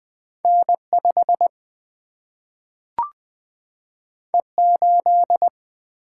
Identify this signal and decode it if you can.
Morse code, heard as a keyed tone, about 20 wpm, 700 Hz. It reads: N 5 E 8
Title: Single-Letter Single-Number 20wpm